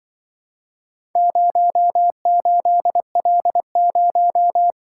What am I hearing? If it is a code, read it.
Morse code, heard as a keyed tone, about 25 wpm, 700 Hz. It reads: 08L0
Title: Speed-Racing - Four Letter-Number Combinations - 20wpm